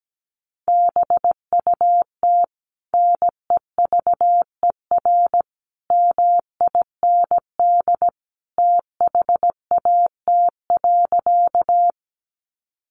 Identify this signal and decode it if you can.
Morse code, heard as a keyed tone, about 17 wpm, 700 Hz. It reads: BUT NEVER MIND THAT.
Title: Sentences from Top 500 Words 17wpm